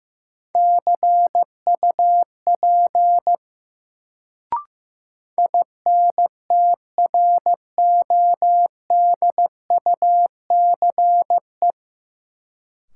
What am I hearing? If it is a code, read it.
Morse code, heard as a keyed tone, about 15 wpm, 700 Hz. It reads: CUP INTRODUCE
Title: Sentences from Top 1600 Words - New Word Drill Alt - 15wpm